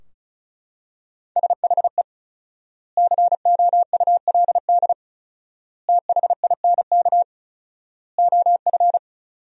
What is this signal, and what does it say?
Morse code, heard as a keyed tone, about 34 wpm, 700 Hz. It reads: SHE COULD THINK OF
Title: 4x Word-Spacing - Sentences from Top 500 Words - 35wpm